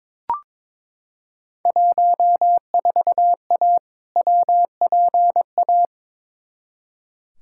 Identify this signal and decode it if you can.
Morse code, heard as a keyed tone, about 22 wpm, 700 Hz. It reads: 14A WPA
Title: Field Day Exchanges Encore - 22wpm